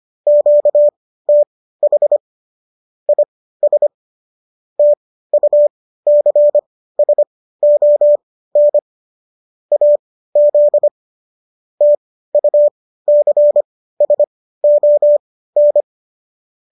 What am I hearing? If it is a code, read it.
Morse code, heard as a keyed tone, about 25 wpm, spaced as for 15 wpm, 590 Hz. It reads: QTH IS TUCSON AZ TUCSON